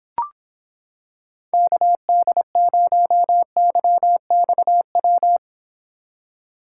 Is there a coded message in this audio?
Yes: Morse code from 1.5 to 5.4 s, heard as a keyed tone, about 26 wpm, 700 Hz. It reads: KD0YXW